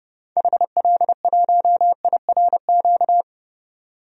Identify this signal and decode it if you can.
Morse code, heard as a keyed tone, about 30 wpm, 700 Hz. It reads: HL1IRQ
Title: DXCC-100 Call Signs - 6 Characters - 30wpm